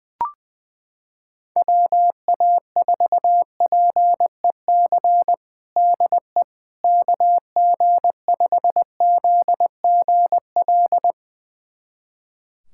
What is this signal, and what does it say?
Morse code, heard as a keyed tone, about 20 wpm, 700 Hz. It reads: WA4PEC DE KG5ZGL
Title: USA Callsign DE Callsign - 6 Characters - 20wpm